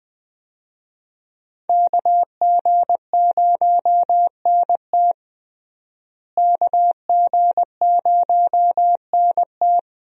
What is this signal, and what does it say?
Morse code, heard as a keyed tone, about 20 wpm, 700 Hz. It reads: KG0NT KG0NT
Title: USA Call Signs - 5 Characters - Repeated 2x - 20wpm